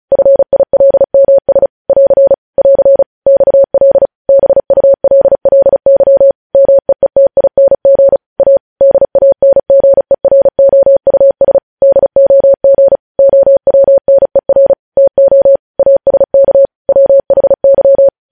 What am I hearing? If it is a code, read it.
Morse code, heard as a keyed tone, about 36 wpm, 550 Hz. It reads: FILMS <AR> <AR> XL BULLY MEETING A DANGEROUS DOG OWNER TO ASK WHY
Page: news headlines in morse code at 35 wpm